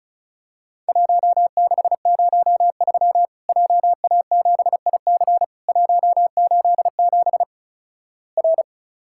Morse code, heard as a keyed tone, about 36 wpm, 700 Hz. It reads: 1603 JA7IC 187 R
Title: Worked All Europe DX Contest - QTC Exchange - 35wpm